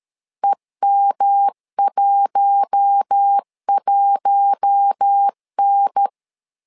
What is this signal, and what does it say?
Morse code, heard as a keyed tone, about 13 wpm, 800 Hz. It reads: EM11N
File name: LEZIONI / 4 Q1 12wpm 2028